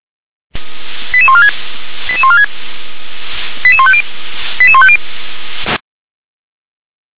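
Five-tone selective call (ZVEI), heard as quick runs of five tones, tones of about 70 ms. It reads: 88146 88146 88150 88150
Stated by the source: ZVEI five-tone selective call received on an SDR.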